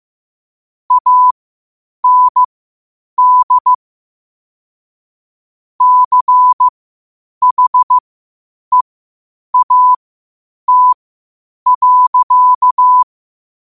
Morse code, heard as a keyed tone, about 15 wpm, spaced as for 8 wpm, 1000 Hz. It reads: AND CHEAT.
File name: qotdfiles / 8wpm